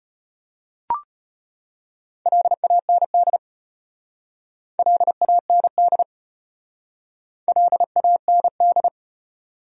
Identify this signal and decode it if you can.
Morse code, heard as a keyed tone, about 33 wpm, 700 Hz. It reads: LAND LAND LAND